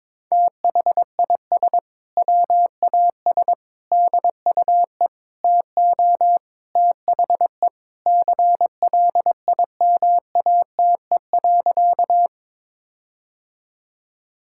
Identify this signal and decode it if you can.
Morse code, heard as a keyed tone, about 22 wpm, 700 Hz. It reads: THIS WAS DUE TO THE CLIMATE.
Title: Sentences from Top 1600 Words 22wpm